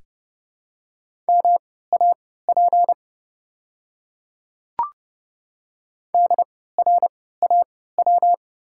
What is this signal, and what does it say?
Morse code, heard as a keyed tone, about 30 wpm, spaced as for 17 wpm, 690 Hz. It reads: MAP DRAW
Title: 3x Character Spacing - Top 500 Words - 30wpm